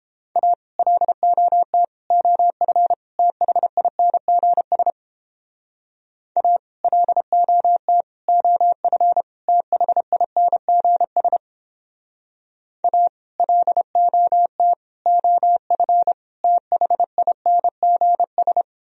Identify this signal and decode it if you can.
Morse code, heard as a keyed tone, about 31 wpm, 700 Hz. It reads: A LOT OF THINGS A LOT OF THINGS A LOT OF THINGS